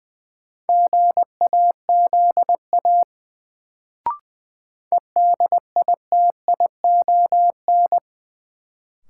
Morse code, heard as a keyed tone, about 20 wpm, 700 Hz. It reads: GAZA EDITION
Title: Sets of 5 Words - New Word Drill - 20wpm